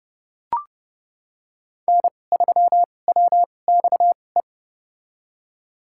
Morse code, heard as a keyed tone, about 30 wpm, spaced as for 22 wpm, 700 Hz. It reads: N3WXE
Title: USA Call Signs - 5 Characters - 2x Character Spacing - 30wpm